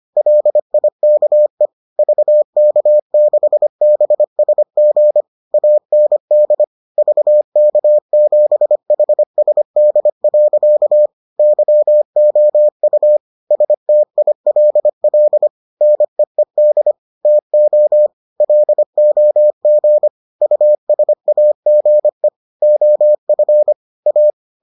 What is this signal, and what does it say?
Morse code, heard as a keyed tone, about 25 wpm, 600 Hz. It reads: LIKE VK6BSG AND VK7HSD. YOU STILL NEED TO LOG USAGE OF A